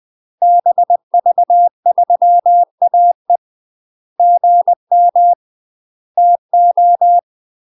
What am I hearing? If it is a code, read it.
Morse code, heard as a keyed tone, about 20 wpm, 700 Hz. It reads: BV3AE GM TO